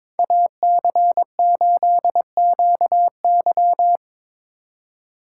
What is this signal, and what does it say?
Morse code, heard as a keyed tone, about 22 wpm, 700 Hz. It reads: AC8QY
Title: USA Call Signs - 5 Characters - 3x Character Spacing - 22wpm